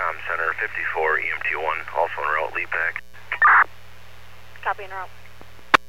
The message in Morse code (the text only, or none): none